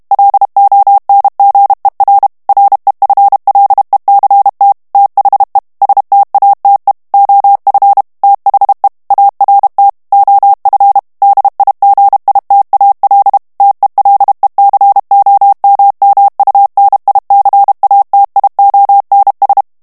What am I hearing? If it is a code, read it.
Morse code, heard as a keyed tone, about 32 wpm, 800 Hz. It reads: LONGER REFLECT THE STATE OF THE ART OF DIGITAL TELECOMMUNICATIONS
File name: ANGLAIS texte clair 3 vitesse 30 mots minute